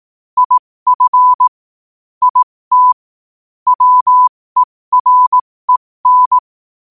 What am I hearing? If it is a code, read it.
Morse code, heard as a keyed tone, about 18 wpm, spaced as for 15 wpm, 1000 Hz. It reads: IF IT WEREN
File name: qotdfiles / 15wpm